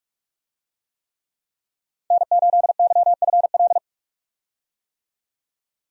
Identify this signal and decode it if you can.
Morse code, heard as a keyed tone, about 42 wpm, 700 Hz. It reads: N8YFL
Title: USA Call Signs - 5 Characters - 45wpm